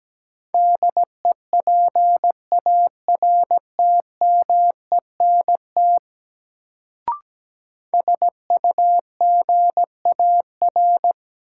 Morse code, heard as a keyed tone, about 17 wpm, 700 Hz. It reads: DEPARTMENT SUGAR
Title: Sentences from Top 1200 Words - New Word Drill Alt - 17wpm